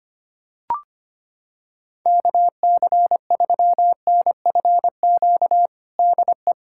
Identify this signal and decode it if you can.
Morse code, heard as a keyed tone, about 25 wpm, 700 Hz. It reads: KC3NFQ DE
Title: USA Callsign DE Callsign - 6 Characters - 25wpm